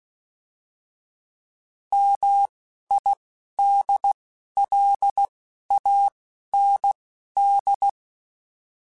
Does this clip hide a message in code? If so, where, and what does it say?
Yes: Morse code from 1.9 to 7.9 s, heard as a keyed tone, about 16 wpm, spaced as for 11 wpm, 800 Hz. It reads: MIDLAND